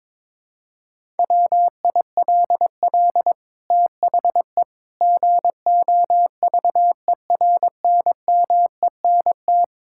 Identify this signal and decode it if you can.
Morse code, heard as a keyed tone, about 21 wpm, 700 Hz. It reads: WILL THE GOVERNMENT
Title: Sentences from Top 1000 Words - Review Part 6 - 22wpm